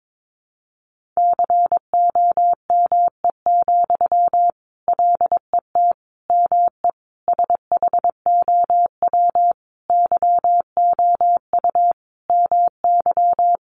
Morse code, heard as a keyed tone, about 22 wpm, 700 Hz. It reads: COME, LET ME SHOW YOU MY
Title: Sentences from Top 500 Words 22wpm